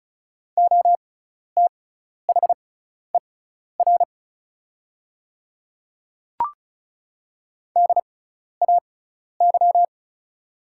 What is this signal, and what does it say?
Morse code, heard as a keyed tone, about 34 wpm, spaced as for 12 wpm, 700 Hz. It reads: OTHER DAY
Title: 6x Character Spacing - Top 100 Words - 35wpm